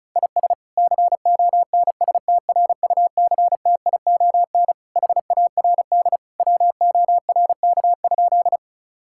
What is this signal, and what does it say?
Morse code, heard as a keyed tone, about 36 wpm, 700 Hz. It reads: IS CONSTRUCTION HARD WORK?
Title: Sentences from Top 1600 Words 35wpm